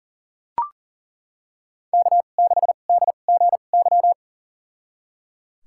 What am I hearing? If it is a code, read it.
Morse code, heard as a keyed tone, about 39 wpm, spaced as for 29 wpm, 700 Hz. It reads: K6DGY
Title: USA Call Signs - 2x Character Spacing - 40wpm